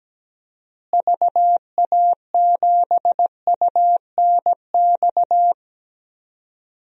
Morse code, heard as a keyed tone, about 17 wpm, 700 Hz. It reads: VA7UNX
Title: DXCC-100 Call Signs - 6 Characters - Repeated 2x - 17wpm